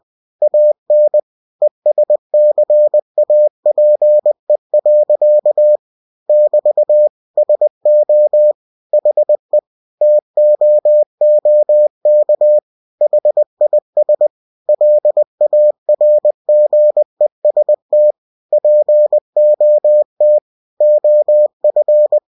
Morse code, heard as a keyed tone, about 20 wpm, 600 Hz. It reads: AN ESCAPE. <BT> SO HE TOOK HIS LARGEST POT OF
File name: cw-books / winnie-the-pooh / winnie-20wpm-page-0094